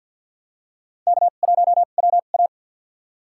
Morse code, heard as a keyed tone, about 50 wpm, spaced as for 36 wpm, 700 Hz. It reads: K1WA
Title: USA Call Signs - 2x Character Spacing - 50wpm